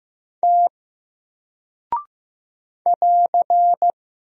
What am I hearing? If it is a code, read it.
Morse code, heard as a keyed tone, about 15 wpm, 700 Hz. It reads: T <AR>
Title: Single Character (Letter:Number:Punctuation:Prosign) 15wpm